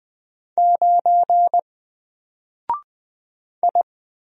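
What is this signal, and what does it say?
Morse code, heard as a keyed tone, about 20 wpm, 700 Hz. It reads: 9 I
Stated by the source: Single letter and number Morse drill at 20 wpm, from N4LCD.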